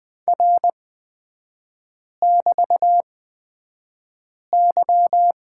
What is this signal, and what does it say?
Morse code, heard as a keyed tone, about 20 wpm, spaced as for 5 wpm, 700 Hz. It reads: R<BT>Y
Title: 224 Lesson 36e - BT - Three Characters 20wpm